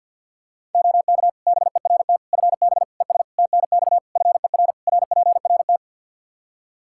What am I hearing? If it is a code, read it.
Morse code, heard as a keyed tone, about 50 wpm, 700 Hz. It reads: OK BERT FB ES TNX FER RPRT
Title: Practice QSO - Exchange 2 - Info - 50wpm